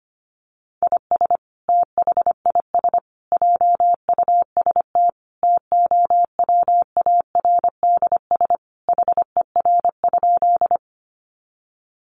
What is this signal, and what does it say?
Morse code, heard as a keyed tone, about 25 wpm, 700 Hz. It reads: IS THIS JUST TOWARDS HER?
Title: Sentences from Top 1100 Words 25wpm